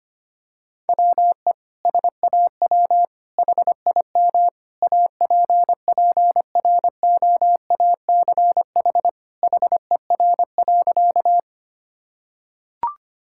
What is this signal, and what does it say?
Morse code, heard as a keyed tone, about 25 wpm, 700 Hz. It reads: WE SAW HIM APPROACH HER.
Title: Sentences from Top 1000 Words - Review Part 6 - 25wpm